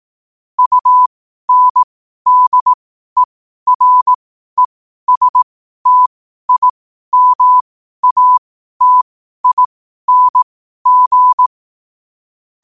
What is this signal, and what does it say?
Morse code, heard as a keyed tone, about 18 wpm, spaced as for 12 wpm, 1000 Hz. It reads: UNDERESTIMATING
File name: qotdfiles / 12wpm